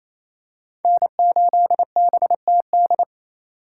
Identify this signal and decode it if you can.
Morse code, heard as a keyed tone, about 28 wpm, 700 Hz. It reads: N8BTD
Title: USA Call Signs - 3x Character Spacing - 28wpm